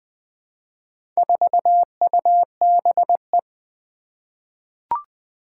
Morse code, heard as a keyed tone, about 20 wpm, 700 Hz. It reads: 4UBE
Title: Four Challenging Letter-Number Combinations 20wpm